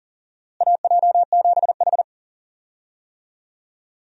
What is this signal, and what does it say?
Morse code, heard as a keyed tone, about 40 wpm, 700 Hz. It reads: AJ7H
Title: USA Call Signs - 4 Characters - Repeated 2x - 40wpm